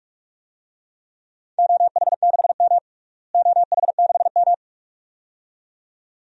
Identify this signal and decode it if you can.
Morse code, heard as a keyed tone, about 41 wpm, 700 Hz. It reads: OH6M OH6M
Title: DXCC-100 Call Signs - 4 Characters - Repeated 2x - 45wpm